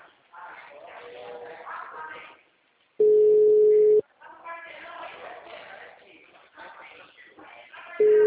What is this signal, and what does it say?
Ringback tone (Europe, 425 Hz), with speech.